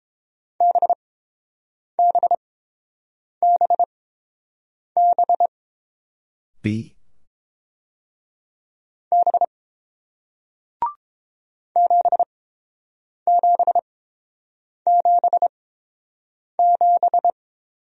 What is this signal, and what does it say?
Morse code, heard as a keyed tone, about 31 wpm, spaced as for 7 wpm, 700 Hz. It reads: BBBB B 7777